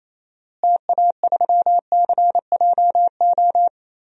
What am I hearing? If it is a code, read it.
Morse code, heard as a keyed tone, about 28 wpm, 700 Hz. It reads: TA3CJO
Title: DXCC-100 Call Signs - 6 Characters - Repeated 3x - 28wpm